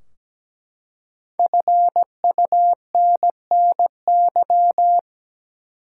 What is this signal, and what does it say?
Morse code, heard as a keyed tone, about 17 wpm, 700 Hz. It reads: FUNNY
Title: Five Letter Words - 17wpm